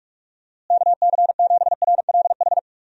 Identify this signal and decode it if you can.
Morse code, heard as a keyed tone, about 41 wpm, 700 Hz. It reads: KC7RLH
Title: USA Call Signs - 6 Characters - Repeated 3x - 45wpm